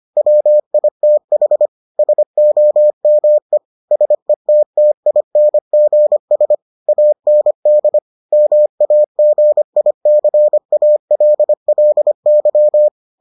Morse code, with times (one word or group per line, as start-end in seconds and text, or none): 0.2-1.7 WITH
2.0-3.6 SOME
3.9-6.5 SETTINGS
6.9-8.0 AND
8.3-12.9 MAGICALLY